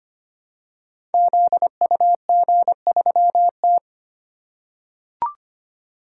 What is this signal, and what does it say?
Morse code, heard as a keyed tone, about 25 wpm, 700 Hz. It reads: ZUG3T
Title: Code Groups - 5 Letter:Numbers - 25wpm